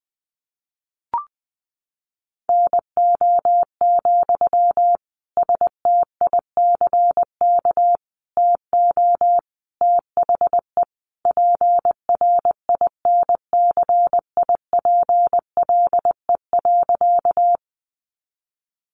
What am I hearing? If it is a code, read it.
Morse code, heard as a keyed tone, about 19 wpm, 700 Hz. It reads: NO, STICK TO THE PRINCIPLE.
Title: Sentences from Top 1400 Words 20wpm